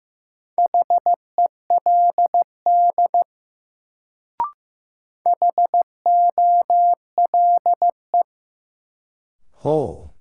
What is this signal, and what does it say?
Morse code, heard as a keyed tone, about 15 wpm, 700 Hz. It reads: HELD HOLE
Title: Sentences from Top 1600 Words - New Word Drill Alt - 15wpm